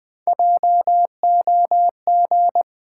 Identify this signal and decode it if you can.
Morse code, heard as a keyed tone, about 21 wpm, 700 Hz. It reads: JOG
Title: Easy Sentences with Hints - 20wpm